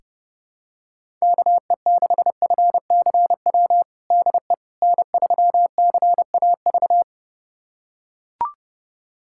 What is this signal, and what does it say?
Morse code, heard as a keyed tone, about 30 wpm, 700 Hz. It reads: KE6FCW DE N3CAV
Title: USA Callsign DE Callsign 30wpm